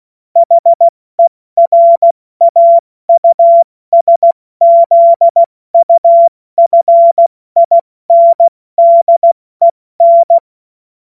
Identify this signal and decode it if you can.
Morse code, heard as a keyed tone, about 16 wpm, 680 Hz. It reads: HERAUSZUFINDEN
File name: Morsekurs / Witze / Witz 21. Dezember 2021, 70 F80